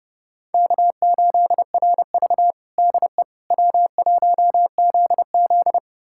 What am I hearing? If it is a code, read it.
Morse code, heard as a keyed tone, about 32 wpm, 700 Hz. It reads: K8RV DE W1ZZ
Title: USA Callsign DE Callsign - 4 Characters - 30wpm